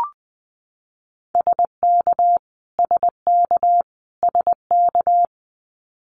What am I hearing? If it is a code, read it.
Morse code, heard as a keyed tone, about 19 wpm, 700 Hz. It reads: SK SK SK
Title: Canadian Provinces and Territories - Repeated 3x - 20wpm